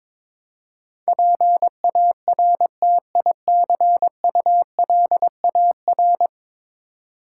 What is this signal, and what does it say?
Morse code, heard as a keyed tone, about 22 wpm, 700 Hz. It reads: PARTICULAR
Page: morse-code-ninja-practice-22wpm